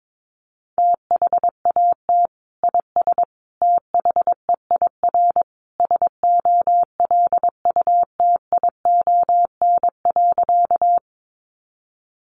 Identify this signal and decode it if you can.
Morse code, heard as a keyed tone, about 22 wpm, 700 Hz. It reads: THAT IS THEIR SOLUTION.
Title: Sentences from Top 1300 Words 22wpm